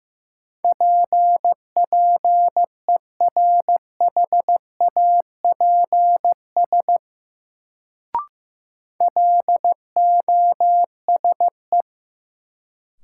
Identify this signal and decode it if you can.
Morse code, heard as a keyed tone, about 15 wpm, 700 Hz. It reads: P PERHAPS LOSE
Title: Sentences from Top 1100 Words - New Word Drill Alt - 15wpm